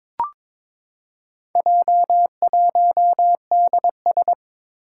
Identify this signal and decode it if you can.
Morse code, heard as a keyed tone, about 22 wpm, 700 Hz. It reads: J1DS
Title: Four Challenging Letter-Number Combinations 22wpm